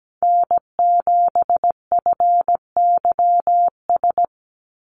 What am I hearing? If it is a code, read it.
Morse code, heard as a keyed tone, about 17 wpm, 700 Hz. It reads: N7FYS
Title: USA Call Signs 17wpm